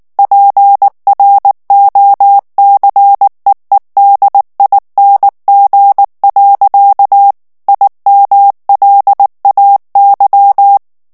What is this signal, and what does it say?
Morse code, heard as a keyed tone, about 19 wpm, 800 Hz. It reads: PROCEEDING. IMLAY
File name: ANGLAIS texte clair 3 vitesse 18 mots minute